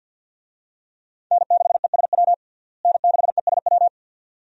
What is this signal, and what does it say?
Morse code, heard as a keyed tone, about 53 wpm, 700 Hz. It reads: N6ESW N6ESW
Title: USA Call Signs - 5 Characters - Repeated 2x - 50wpm